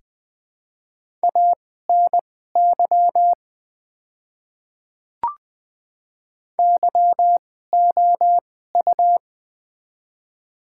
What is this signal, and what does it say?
Morse code, heard as a keyed tone, about 20 wpm, spaced as for 14 wpm, 700 Hz. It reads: ANY YOU
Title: 2x Character Spacing - Top 200 Words - 20wpm